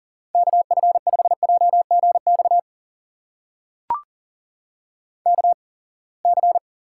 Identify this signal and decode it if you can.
Morse code, heard as a keyed tone, about 40 wpm, spaced as for 27 wpm, 700 Hz. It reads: KF5JGX KC
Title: USA Call Signs - 6 Characters - 8x Character Spacing - 40wpm